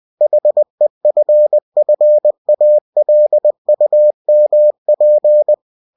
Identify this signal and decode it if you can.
Morse code, heard as a keyed tone, about 20 wpm, 600 Hz. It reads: HEFFALUMP
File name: cw-books / winnie-the-pooh / winnie-20wpm-page-0041